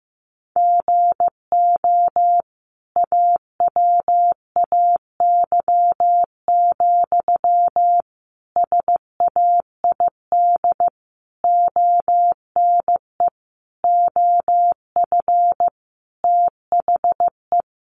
Morse code, heard as a keyed tone, about 14 wpm, 700 Hz. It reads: GO AWAY, SAID ONE OF THE
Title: Sentences from Top 1500 Words 15wpm